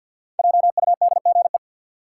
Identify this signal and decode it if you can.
Morse code, heard as a keyed tone, about 47 wpm, 700 Hz. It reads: JUDGE
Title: Five Letter Words - 50wpm